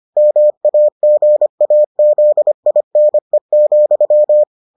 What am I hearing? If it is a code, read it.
Morse code, heard as a keyed tone, about 25 wpm, 600 Hz. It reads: MAGAZINE,